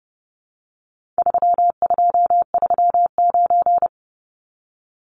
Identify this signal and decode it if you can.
Morse code, heard as a keyed tone, about 28 wpm, 700 Hz. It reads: 3239
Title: Numbers - 4-Digit 30wpm